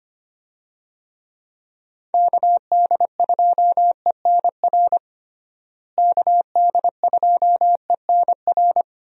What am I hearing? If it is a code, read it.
Morse code, heard as a keyed tone, about 25 wpm, 700 Hz. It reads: KD2ENR KD2ENR
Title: USA Call Signs - 6 Characters - Repeated 2x - 25wpm